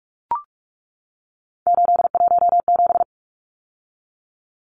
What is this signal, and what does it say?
Morse code, heard as a keyed tone, about 44 wpm, 700 Hz. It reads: 817
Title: Numbers - 3-Digit 45wpm